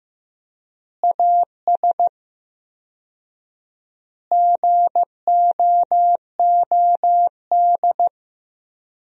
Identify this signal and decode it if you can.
Morse code, heard as a keyed tone, about 15 wpm, 700 Hz. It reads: AS GOOD
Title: 4x Word-Spacing - Sentences from Top 200 Words - 15wpm